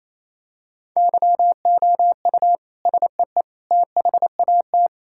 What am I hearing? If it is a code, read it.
Morse code, heard as a keyed tone, about 27 wpm, 700 Hz. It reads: YOU SEE THAT
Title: Sentences from Top 1000 Words - Review Part 6 - 28wpm